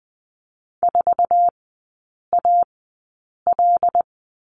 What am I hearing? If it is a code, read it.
Morse code, heard as a keyed tone, about 20 wpm, spaced as for 8 wpm, 700 Hz. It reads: 4AL
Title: 074 Lesson 13e - Letter L - Three Characters 20wpm